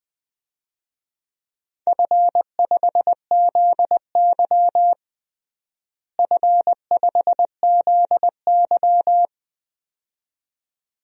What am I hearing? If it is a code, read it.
Morse code, heard as a keyed tone, about 20 wpm, 700 Hz. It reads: F5ZY F5ZY
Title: DXCC-100 Call Signs - 4 Characters - Repeated 2x - 20wpm